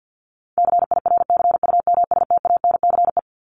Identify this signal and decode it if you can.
Morse code, heard as a keyed tone, about 50 wpm, 700 Hz. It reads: CIRCUMSTANCE
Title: ES:S Suffix - Top 100 Words with Base Words 50wpm